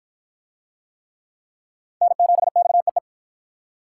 Morse code, heard as a keyed tone, about 53 wpm, 700 Hz. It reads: N7XEE